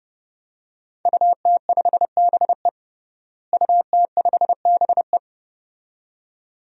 Morse code, heard as a keyed tone, about 31 wpm, 700 Hz. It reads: UT5BE UT5BE
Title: DXCC-100 Call Signs - 5 Characters - Repeated 2x - 30wpm